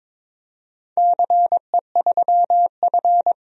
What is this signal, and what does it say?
Morse code, heard as a keyed tone, about 22 wpm, 700 Hz. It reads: CE3F
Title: DXCC-100 Call Signs - 4 Characters - 22wpm